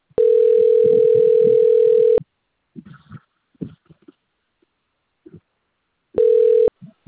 Ringback tone (North America), 440 + 480 Hz.